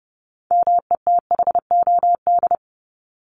Morse code, heard as a keyed tone, about 30 wpm, 700 Hz. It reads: METHOD
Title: Top 501-750 Words 30wpm